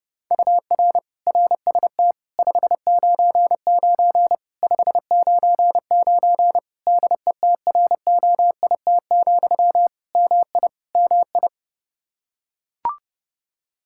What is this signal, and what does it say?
Morse code, heard as a keyed tone, about 30 wpm, 700 Hz. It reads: UR RST 599 599 DETROIT, MI MI
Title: Practice QSO - Exchange 1 - Beginning - 30wpm